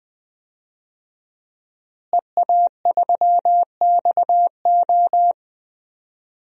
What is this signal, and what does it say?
Morse code, heard as a keyed tone, about 20 wpm, 700 Hz. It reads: EA3XO
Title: DXCC-100 Call Signs - 5 Characters - Repeated 3x - 20wpm